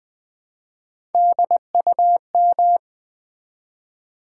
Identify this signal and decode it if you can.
Morse code, heard as a keyed tone, about 20 wpm, 700 Hz. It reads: DUM
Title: Build A Word - 1 Letter At A Time - 4-Letter Words - 20wpm